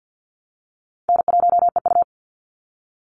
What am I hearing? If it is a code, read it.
Morse code, heard as a keyed tone, about 48 wpm, 700 Hz. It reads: N1EU
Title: NA SOTA Activator Callsigns 50wpm